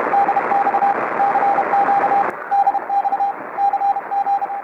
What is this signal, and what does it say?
Morse code, heard as a keyed tone, about 32 wpm, 800 Hz, with radio noise. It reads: DX CQ DX CG
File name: 20m CW